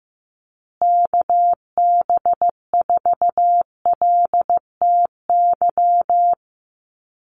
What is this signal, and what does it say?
Morse code, heard as a keyed tone, about 15 wpm, 700 Hz. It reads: KB4LTY
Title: USA Call Signs Repeated 2x 15wpm